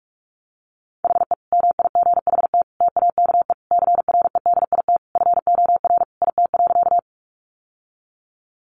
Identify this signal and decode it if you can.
Morse code, heard as a keyed tone, about 44 wpm, 700 Hz. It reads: HE MIGHT TAKE CREDIT FOR IT.
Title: Sentences from Top 1000 Words 45wpm